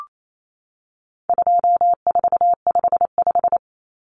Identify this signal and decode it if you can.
Morse code, heard as a keyed tone, about 28 wpm, 700 Hz. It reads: 2455
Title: Common Cut Numbers - 4-Digit 28wpm